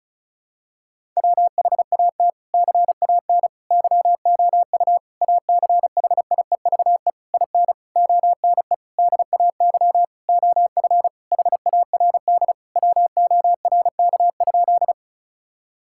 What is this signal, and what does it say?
Morse code, heard as a keyed tone, about 36 wpm, 700 Hz. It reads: WHAT CAN YOU ACHIEVE IN ONE DAY OF HARD WORK?
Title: Sentences from Top 1000 Words - Review Part 6 - 35wpm